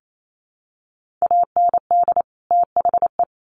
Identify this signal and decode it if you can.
Morse code, heard as a keyed tone, about 28 wpm, 700 Hz. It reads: AND THE
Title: Sets of 2 Words 28wpm